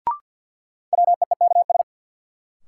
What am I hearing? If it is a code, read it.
Morse code, heard as a keyed tone, about 53 wpm, 700 Hz. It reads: WEEKS
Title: Sets of 5 Words - New Word Drill - 50wpm